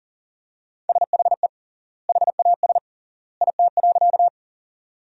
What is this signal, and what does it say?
Morse code, heard as a keyed tone, about 40 wpm, 700 Hz. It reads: SHE HAS IT.
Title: 3x Word-Spacing - Sentences from Top 100 Words - 40wpm